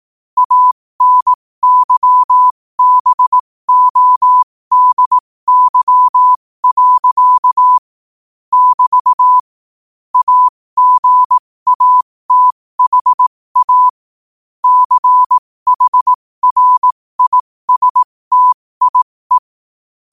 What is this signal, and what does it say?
Morse code, heard as a keyed tone, about 18 wpm, spaced as for 15 wpm, 1000 Hz. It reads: ANYBODY. <BT> AGATHA CHRISTIE
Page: 15 WPM morse code quotes